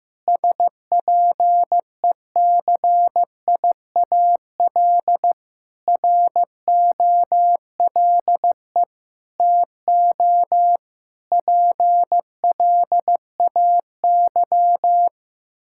Morse code, heard as a keyed tone, about 15 wpm, 700 Hz. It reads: SPECIAL ROLE TO PLAY
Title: Sentences from Top 1000 Words - Review Part 6 - 15wpm